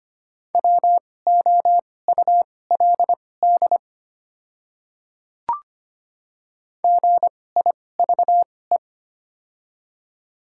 Morse code, heard as a keyed tone, about 25 wpm, spaced as for 18 wpm, 700 Hz. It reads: WOULD GIVE
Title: 2x Character Spacing - Top 200 Words - 25wpm